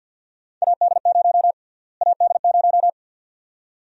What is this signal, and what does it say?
Morse code, heard as a keyed tone, about 52 wpm, 700 Hz. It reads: AD0 AD0
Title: USA Call Sign Prefixes - Repeated 2x - 50wpm